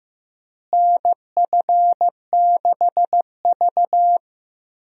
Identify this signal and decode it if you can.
Morse code, heard as a keyed tone, about 15 wpm, 700 Hz. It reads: NF6V
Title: USA Call Signs - 4 Characters - Repeated 3x - 15wpm